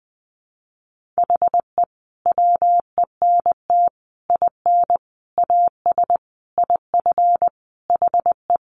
Morse code, heard as a keyed tone, about 20 wpm, 700 Hz. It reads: HE WENT IN AS IF HE
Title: Sentences from Top 1400 Words 20wpm